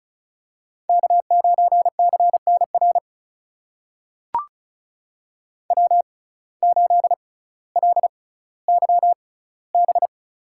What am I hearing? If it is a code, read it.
Morse code, heard as a keyed tone, about 36 wpm, spaced as for 18 wpm, 700 Hz. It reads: K9CNR W8LYB
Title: USA Call Signs - 5 Characters - 6x Character Spacing - 35wpm